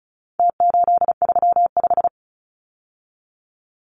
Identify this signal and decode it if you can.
Morse code, heard as a keyed tone, about 34 wpm, 700 Hz. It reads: T835
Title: Numbers - Contest Style - 4-Digit 35wpm